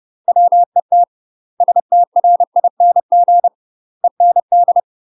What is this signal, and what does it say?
Morse code, heard as a keyed tone, about 30 wpm, 700 Hz. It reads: WET STRING END
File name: UK MASTER QSOs TRAINING / 30WPM / UK MASTER QSOs 0000